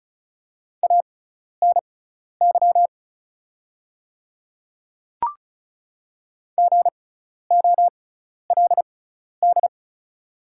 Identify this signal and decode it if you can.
Morse code, heard as a keyed tone, about 34 wpm, spaced as for 12 wpm, 690 Hz. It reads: ANY GOLD
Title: 6x Character Spacing - Top 500 Words - 35wpm